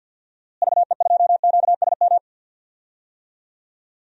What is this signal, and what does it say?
Morse code, heard as a keyed tone, about 52 wpm, 700 Hz. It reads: VE2QSM
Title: DXCC-100 Call Signs - 6 Characters - Repeated 2x - 50wpm